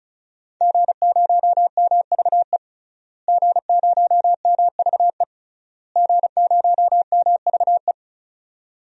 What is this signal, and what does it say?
Morse code, heard as a keyed tone, about 36 wpm, 700 Hz. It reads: G0MVE G0MVE G0MVE
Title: DXCC-100 Call Signs - 5 Characters - Repeated 3x - 35wpm